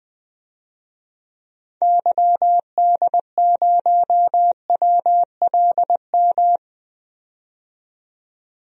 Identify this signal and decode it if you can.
Morse code, heard as a keyed tone, about 20 wpm, 700 Hz. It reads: YD0WLM
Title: DXCC-100 Call Signs - 6 Characters - 20wpm